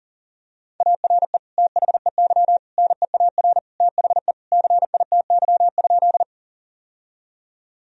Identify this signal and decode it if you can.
Morse code, heard as a keyed tone, about 40 wpm, 700 Hz. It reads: ARE THEY NEAR THE CITY?
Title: Sentences from Top 1000 Words - Review Part 1 - 40wpm